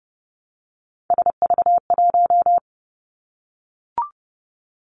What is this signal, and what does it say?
Morse code, heard as a keyed tone, about 30 wpm, 700 Hz. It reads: SV1
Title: Callsign Prefixes for Top 50 DXCC - 30wpm